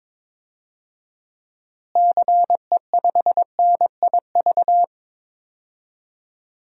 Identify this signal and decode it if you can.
Morse code, heard as a keyed tone, about 22 wpm, 700 Hz. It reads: CE5NIV
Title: DXCC-100 Call Signs - 6 Characters - Repeated 3x - 22wpm